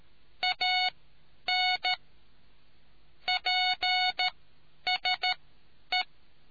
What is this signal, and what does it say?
Morse code, heard as a keyed tone, about 13 wpm, spaced as for 9 wpm, 740 Hz. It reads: AN PSE